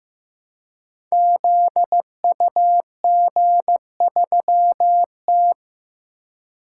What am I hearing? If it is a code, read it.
Morse code, heard as a keyed tone, about 15 wpm, 700 Hz. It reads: ZUG3T
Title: Code Groups - 5 Letter:Numbers - 15wpm